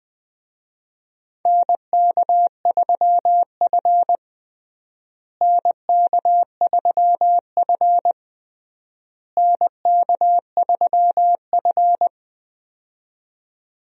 Morse code, heard as a keyed tone, about 20 wpm, 700 Hz. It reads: NK3F NK3F NK3F
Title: USA Call Signs - 4 Characters - Repeated 3x - 20wpm